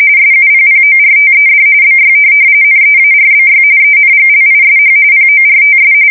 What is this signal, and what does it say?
RTTY, about 45.45 baud. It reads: SUPPORTED BY NEARBY ENGLISH PORTS
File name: fsk170 / USS-New-Jersey